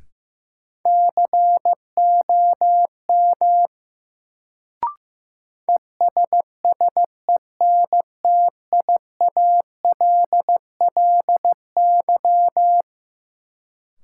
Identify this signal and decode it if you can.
Morse code, heard as a keyed tone, about 15 wpm, 700 Hz. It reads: COM ESSENTIALLY
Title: Sentences from Top 1300 Words - New Word Drill - 15wpm